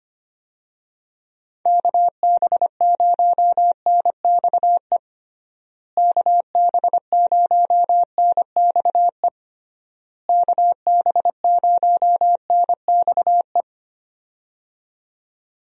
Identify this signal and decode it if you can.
Morse code, heard as a keyed tone, about 25 wpm, 700 Hz. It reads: KB0NXE KB0NXE KB0NXE
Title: USA Call Signs - 6 Characters - Repeated 3x - 25wpm